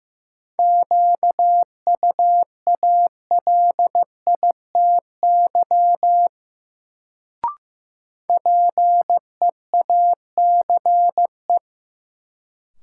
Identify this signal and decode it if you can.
Morse code, heard as a keyed tone, about 16 wpm, 700 Hz. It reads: QUALITY PEACE
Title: Sentences from Top 1500 Words - New Word Drill Alt - 15wpm